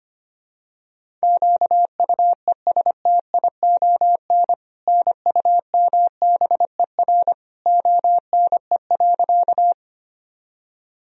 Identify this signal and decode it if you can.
Morse code, heard as a keyed tone, about 25 wpm, 700 Hz. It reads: QUESTION NUMBER ONE.
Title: Sentences from Top 1000 Words - Review Part 1 - 25wpm